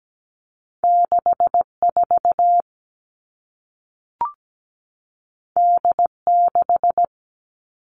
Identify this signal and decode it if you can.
Morse code, heard as a keyed tone, about 17 wpm, 700 Hz. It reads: 64 D6
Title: All Cut Numbers - 2-Digit 17wpm